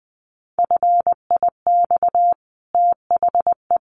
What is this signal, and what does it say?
Morse code, heard as a keyed tone, about 20 wpm, 700 Hz. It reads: FIX THE
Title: Sentences from Top 1400 Words 20wpm